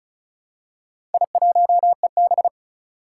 Morse code, heard as a keyed tone, about 36 wpm, 700 Hz. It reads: I1EB
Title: Four Challenging Letter-Number Combinations 35wpm